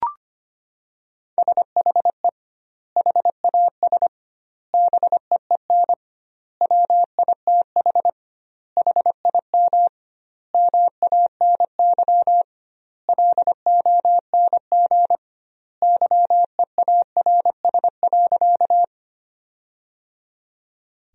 Morse code, heard as a keyed tone, about 25 wpm, 700 Hz. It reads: SHE HAS BEEN WITH HIM MANY LONG YEARS.
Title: 2x Word-Spacing - Sentences from Top 200 Words - 25wpm